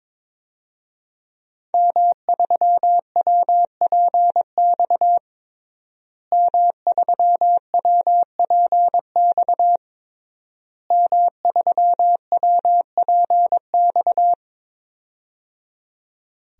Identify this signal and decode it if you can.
Morse code, heard as a keyed tone, about 22 wpm, 700 Hz. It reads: M3WPX M3WPX M3WPX
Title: DXCC-100 Call Signs - 5 Characters - Repeated 3x - 22wpm